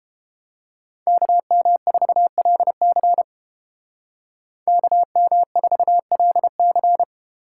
Morse code, heard as a keyed tone, about 30 wpm, 700 Hz. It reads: KM4LC KM4LC